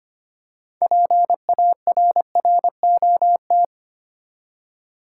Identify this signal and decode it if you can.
Morse code, heard as a keyed tone, about 25 wpm, 690 Hz. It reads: PARROT
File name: practice / 25wpm / Easy Words with Hints - 25wpm